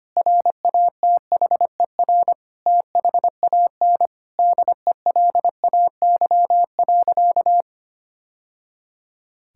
Morse code, heard as a keyed tone, about 25 wpm, 700 Hz. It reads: RATHER THAN DELAY.
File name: practice / 25wpm / Sentences with 2-Word Phrases - 25wpm